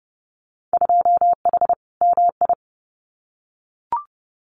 Morse code, heard as a keyed tone, about 32 wpm, 700 Hz. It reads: 2H MI
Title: Winter Field Day Exchanges Encore 30wpm